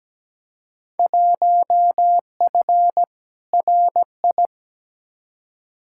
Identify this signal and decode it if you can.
Morse code, heard as a keyed tone, about 17 wpm, 700 Hz. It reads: 1F RI
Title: Field Day Exchanges Encore - 17wpm